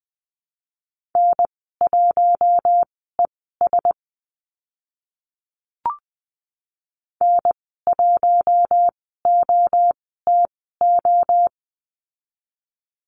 Morse code, heard as a keyed tone, about 20 wpm, spaced as for 14 wpm, 700 Hz. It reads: N1ES N1OTO
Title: 042 Lesson 8g - Number 1 - Call Signs 20wpm